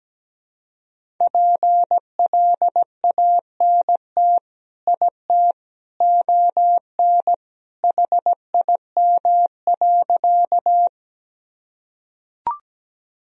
Morse code, heard as a keyed tone, about 17 wpm, 700 Hz. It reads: PLANT IT ON HIM.
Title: Sentences from Top 1000 Words - Review Part 1 - 17wpm